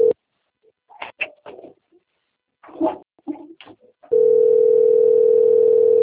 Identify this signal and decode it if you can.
Ringback tone (North America), 440 + 480 Hz.